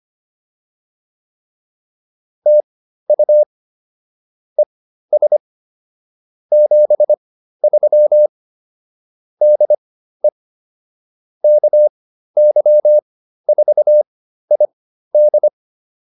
Morse code, heard as a keyed tone, about 25 wpm, spaced as for 13 wpm, 600 Hz. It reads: TU ES 73 DE KY4ID